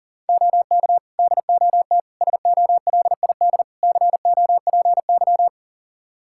Morse code, heard as a keyed tone, about 37 wpm, 690 Hz. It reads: OK DOT SOLID COPY
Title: Practice QSO - Exchange 3 - More Info - 40wpm